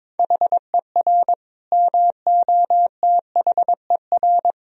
Morse code, heard as a keyed tone, about 22 wpm, 700 Hz. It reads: HER MOTHER
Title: Sentences from Top 1000 Words - Review Part 4 - 22wpm